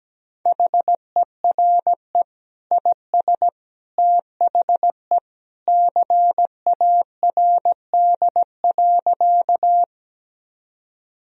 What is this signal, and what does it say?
Morse code, heard as a keyed tone, about 17 wpm, 700 Hz. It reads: HERE IS THE CARD.
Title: Sentences from Top 1000 Words - Review Part 4 - 17wpm